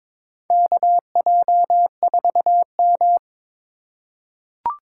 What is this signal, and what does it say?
Morse code, heard as a keyed tone, about 23 wpm, 700 Hz. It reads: KJ4M
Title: USA Call Signs - 4 Characters - 3x Character Spacing - 22wpm